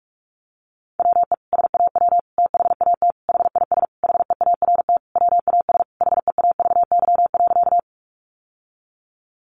Morse code, heard as a keyed tone, about 44 wpm, 700 Hz. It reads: WE SAW THAT HIS HEART WAS HEAVY.
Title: Sentences from Top 1200 Words 45wpm